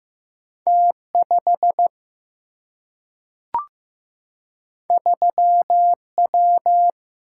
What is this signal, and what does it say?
Morse code, heard as a keyed tone, about 15 wpm, 700 Hz. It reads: T 5 3 W
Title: Single-Letter Single-Number 15wpm